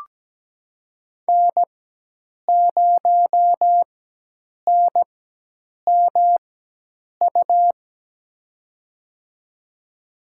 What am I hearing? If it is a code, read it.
Morse code, heard as a keyed tone, about 17 wpm, spaced as for 8 wpm, 700 Hz. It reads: N0NMU
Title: USA Call Signs - 5 Characters - 4x Character Spacing - 17wpm